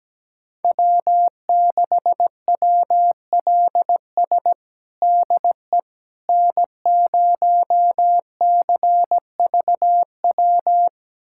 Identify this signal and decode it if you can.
Morse code, heard as a keyed tone, about 17 wpm, 700 Hz. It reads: W6WLS DE N0CVW
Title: USA Callsign DE Callsign - 5 Characters - 17wpm